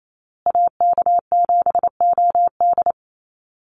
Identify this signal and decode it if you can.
Morse code, heard as a keyed tone, about 28 wpm, 700 Hz. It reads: AK7OD
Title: USA Call Signs Repeated 3x 28wpm